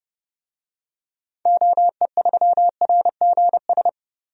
Morse code, heard as a keyed tone, about 30 wpm, 700 Hz. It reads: OE3RGS
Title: DXCC-100 Call Signs - 6 Characters - 30wpm